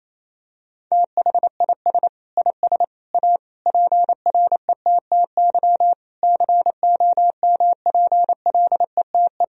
Morse code, heard as a keyed tone, about 28 wpm, 700 Hz. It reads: THIS IS A PRETTY COMPLETE
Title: Sentences from Top 1000 Words - Review Part 2 - 28wpm